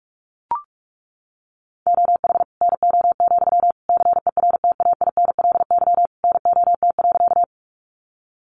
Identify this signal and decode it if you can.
Morse code, heard as a keyed tone, about 44 wpm, 700 Hz. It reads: OH NO, CERTAINLY NOT.
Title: Sentences from Top 900 Words 45wpm